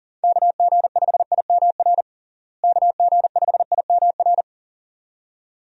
Morse code, heard as a keyed tone, about 37 wpm, 700 Hz. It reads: KG5IMR KG5IMR
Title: USA Call Signs - 6 Characters - Repeated 2x - 40wpm